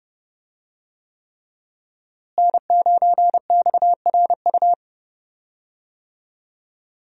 Morse code, heard as a keyed tone, about 30 wpm, 700 Hz. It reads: N9XRU
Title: USA Call Signs - 5 Characters - Repeated 3x - 30wpm